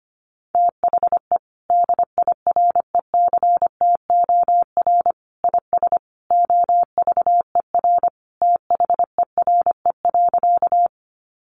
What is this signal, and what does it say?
Morse code, heard as a keyed tone, about 25 wpm, 700 Hz. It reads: THE DIRECTOR IS OVER THERE.
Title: Sentences from Top 1300 Words 25wpm